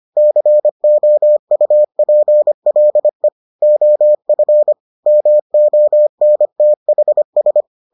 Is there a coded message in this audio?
Yes: Morse code from 0.2 to 7.6 s, heard as a keyed tone, about 25 wpm, 600 Hz. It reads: COUPLE OF MONTHS